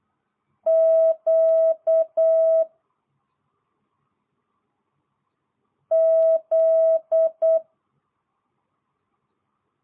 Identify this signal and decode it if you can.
Morse code, heard as a keyed tone, about 8 wpm, 650 Hz. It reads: Q Z